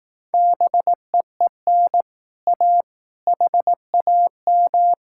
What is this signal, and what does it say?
Morse code, heard as a keyed tone, about 18 wpm, 700 Hz. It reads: BEEN A HAM
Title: Speed-Racing Common QSO Phrases Encore - 15wpm